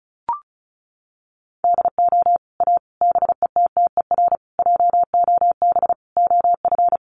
Morse code, heard as a keyed tone, about 34 wpm, 700 Hz. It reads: DO A BETTER JOB OF
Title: Sets of 5 Words Encore2 35wpm